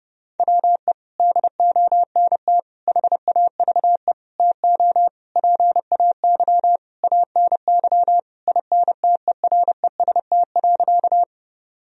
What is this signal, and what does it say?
Morse code, heard as a keyed tone, about 30 wpm, 700 Hz. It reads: WE DONT HAVE TO PAY ANY INTEREST.
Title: Sentences from Top 1000 Words - Review Part 2 - 30wpm